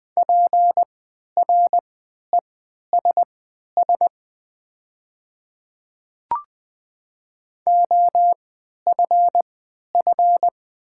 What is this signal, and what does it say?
Morse code, heard as a keyed tone, about 20 wpm, spaced as for 11 wpm, 700 Hz. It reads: PRESS OFF
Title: 3x Character Spacing - Top 300 Words - 20wpm